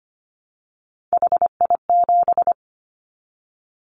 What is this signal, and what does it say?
Morse code, heard as a keyed tone, about 25 wpm, 700 Hz. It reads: HI7
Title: Callsign Prefixes for Top 100 DXCC - 25wpm